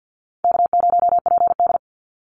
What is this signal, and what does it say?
Morse code, heard as a keyed tone, about 50 wpm, 700 Hz. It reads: K0PD
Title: Easy Call Signs - 50wpm